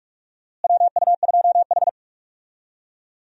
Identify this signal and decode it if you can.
Morse code, heard as a keyed tone, about 42 wpm, 700 Hz. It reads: WU2H
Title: USA Call Signs - 4 Characters - Repeated 3x - 45wpm